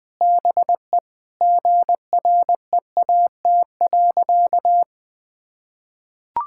Morse code, heard as a keyed tone, about 21 wpm, 700 Hz. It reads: BE GREAT.
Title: Sentences with 2-Word Phrases - 20wpm